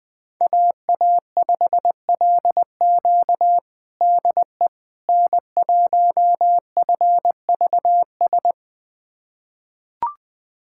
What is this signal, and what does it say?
Morse code, heard as a keyed tone, about 20 wpm, 700 Hz. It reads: AA5LQ DE N1FVS
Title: USA Callsign DE Callsign - 5 Characters - 20wpm